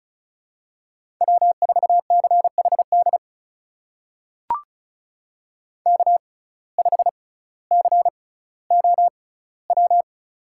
Morse code, heard as a keyed tone, about 36 wpm, spaced as for 18 wpm, 700 Hz. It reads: W4CHD K5COW
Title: USA Call Signs - 5 Characters - 6x Character Spacing - 35wpm